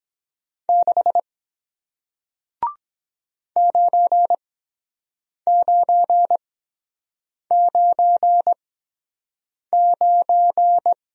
Morse code, heard as a keyed tone, about 23 wpm, spaced as for 7 wpm, 700 Hz. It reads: 6 9999